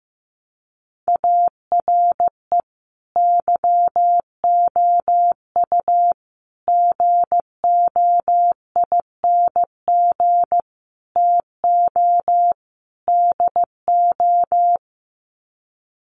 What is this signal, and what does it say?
Morse code, heard as a keyed tone, about 15 wpm, 700 Hz. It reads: ARE YOU GOING TO DO
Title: Sets of 5 Words 15wpm